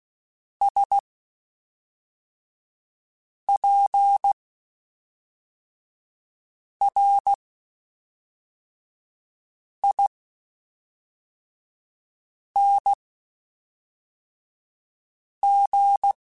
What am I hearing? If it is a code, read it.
Morse code, heard as a keyed tone, about 16 wpm, spaced as for 3 wpm, 800 Hz. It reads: SPRING
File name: (spaced) / QSOs Spaced-mp3s / WA7QOX 1 @5 sp